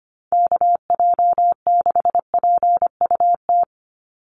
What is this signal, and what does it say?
Morse code, heard as a keyed tone, about 25 wpm, 700 Hz. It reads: KJ6PUT
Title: USA Call Signs 25wpm